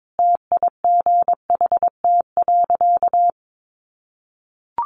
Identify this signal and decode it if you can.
Morse code, heard as a keyed tone, about 22 wpm, 700 Hz. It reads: TIGHT.
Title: A Sound Of Thunder 22wpm